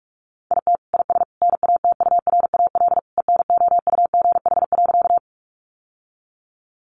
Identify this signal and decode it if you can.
Morse code, heard as a keyed tone, about 44 wpm, 700 Hz. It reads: IT IS NATURAL ENOUGH.
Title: Sentences from Top 700 Words 45wpm